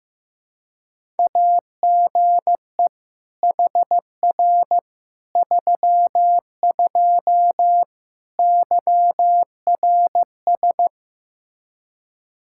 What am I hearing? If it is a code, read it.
Morse code, heard as a keyed tone, about 15 wpm, 700 Hz. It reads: AGE HR 32 YRS
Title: Practice QSO - Exchange 3 - More Info - 15wpm